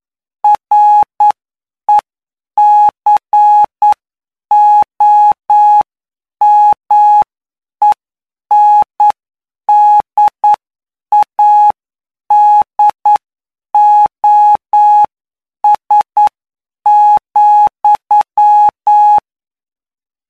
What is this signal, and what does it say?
Morse code, heard as a keyed tone, about 9 wpm, spaced as for 7 wpm, 840 Hz. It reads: RECOMENDADOS,
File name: cw / py2lcd / 20-PY2LCD-Li��o 29